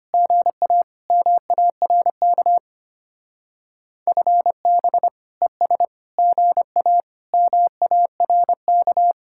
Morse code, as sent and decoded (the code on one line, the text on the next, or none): --. .- / -- .- .-. -.- / ..-. -... / . ... / --. .- / -- .- .-. -.-
GA MARK FB ES GA MARK